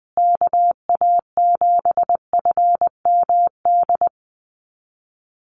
Morse code, heard as a keyed tone, about 20 wpm, 700 Hz. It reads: KA7FMD
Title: USA Call Signs Repeated 2x 20wpm